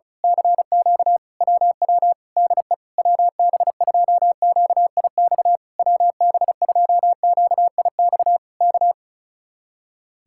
Morse code, heard as a keyed tone, about 33 wpm, 700 Hz. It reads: CQ WW DE WB2QIX WB2QIX K
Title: Speed-Racing Common QSO Phrases Encore - 35wpm